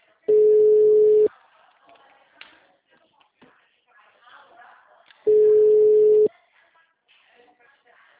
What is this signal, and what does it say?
Ringback tone (Europe, 425 Hz), with speech.